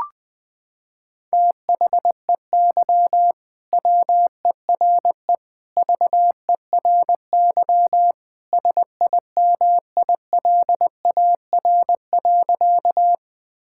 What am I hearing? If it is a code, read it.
Morse code, heard as a keyed tone, about 21 wpm, 700 Hz. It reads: THEY WERE VERY SIMILAR.
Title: Sentences from Top 1000 Words - Review Part 5 - 20wpm